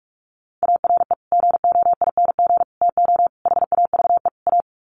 Morse code, heard as a keyed tone, about 44 wpm, 700 Hz. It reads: ARE GOING TO HAVE A
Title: Sets of 5 Words Encore2 45wpm